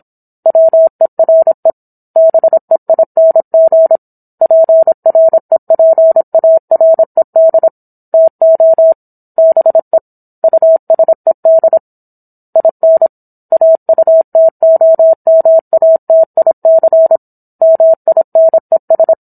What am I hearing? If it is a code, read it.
Morse code, heard as a keyed tone, about 27 wpm, 650 Hz. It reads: WERE BEING PREPARED TO BE USED IN AUTOMATIC MINES
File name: Book One / 17-Chapter 17 - 26 WPM